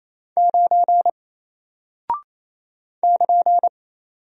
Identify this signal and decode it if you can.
Morse code, heard as a keyed tone, about 28 wpm, 700 Hz. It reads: 9 <KN>
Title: Single Character (Letter:Number:Punctuation:Prosign) Alt 28wpm